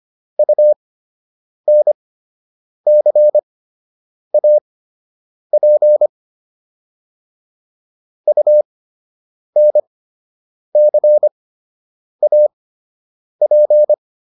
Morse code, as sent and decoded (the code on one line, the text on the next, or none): ..- -. -.-. .- .--. / ..- -. -.-. .- .--.
UNCAP UNCAP